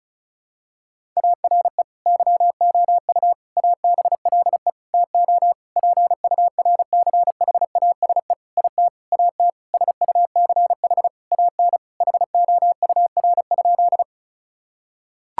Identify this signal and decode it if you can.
Morse code, heard as a keyed tone, about 36 wpm, 700 Hz. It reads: ARE YOU ABLE TO PURCHASE IT AT SUCH AN HOUR?
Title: Sentences from Top 1000 Words - Review Part 5 - 35wpm